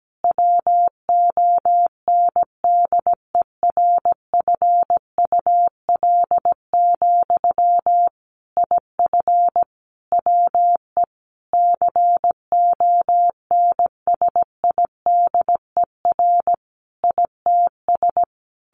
Morse code, heard as a keyed tone, about 17 wpm, 700 Hz. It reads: WONDERFUL, IF WE CONSIDER ITS
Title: Sentences from Top 1100 Words 17wpm